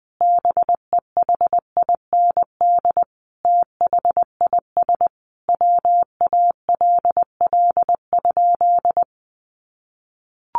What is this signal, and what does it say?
Morse code, heard as a keyed tone, about 20 wpm, 700 Hz. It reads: BEHIND THIS WALL?
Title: Sentences from Top 1100 Words 20wpm